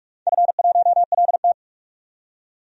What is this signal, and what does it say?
Morse code, heard as a keyed tone, about 47 wpm, 700 Hz. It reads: F1LT
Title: DXCC-100 Call Signs - 4 Characters - 45wpm